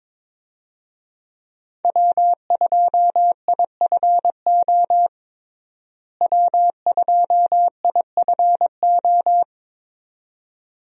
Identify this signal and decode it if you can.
Morse code, heard as a keyed tone, about 22 wpm, 700 Hz. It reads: W2IFO W2IFO
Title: USA Call Signs - 5 Characters - Repeated 2x - 22wpm